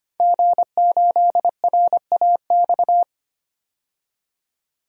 Morse code, heard as a keyed tone, about 25 wpm, 700 Hz. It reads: G8RAX
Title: DXCC-100 Call Signs - 5 Characters - 25wpm